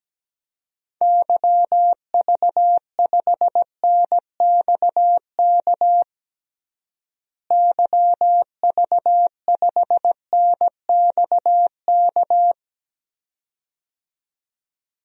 Morse code, heard as a keyed tone, about 17 wpm, 700 Hz. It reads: YV5NXK YV5NXK
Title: DXCC-100 Call Signs - 6 Characters - Repeated 2x - 17wpm